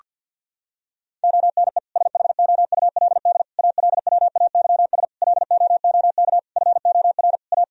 Morse code, heard as a keyed tone, about 51 wpm, 700 Hz. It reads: ONE SHOULD ALWAYS LOOK FOR A
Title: Sentences from Top 1000 Words - Review Part 4 - 50wpm